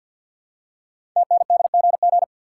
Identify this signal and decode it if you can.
Morse code, heard as a keyed tone, about 48 wpm, 700 Hz. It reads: TNDGG
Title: Code Groups - 5 Letters - 50wpm